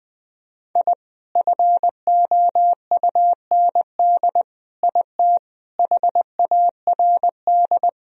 Morse code, heard as a keyed tone, about 20 wpm, 700 Hz. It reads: I FOUND IT HARD
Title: Sentences from Top 1000 Words - Review Part 1 - 20wpm